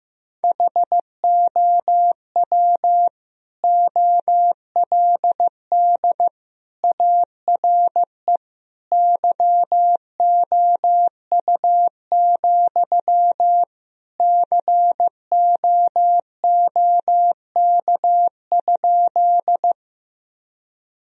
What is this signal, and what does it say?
Morse code, heard as a keyed tone, about 15 wpm, 700 Hz. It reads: HOW OLD ARE YOU, COOK?
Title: Sentences from Top 1000 Words - Review Part 4 - 15wpm